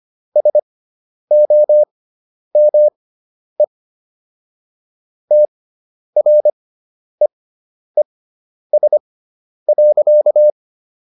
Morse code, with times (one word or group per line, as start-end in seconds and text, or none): 0.4-3.6 SOME
5.3-10.5 TREES.